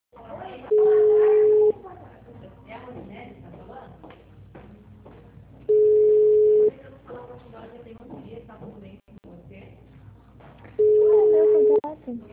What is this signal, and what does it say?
Ringback tone (Europe, 425 Hz), with speech.